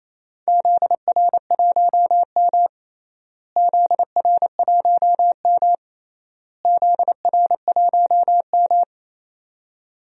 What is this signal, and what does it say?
Morse code, heard as a keyed tone, about 27 wpm, 700 Hz. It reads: ZR1M ZR1M ZR1M
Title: DXCC-100 Call Signs - 4 Characters - Repeated 3x - 28wpm